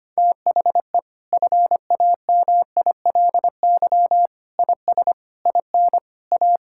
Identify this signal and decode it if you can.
Morse code, heard as a keyed tone, about 25 wpm, 700 Hz. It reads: THE FAMILY IS IN A
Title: Sentences from Top 1000 Words - Review Part 5 - 25wpm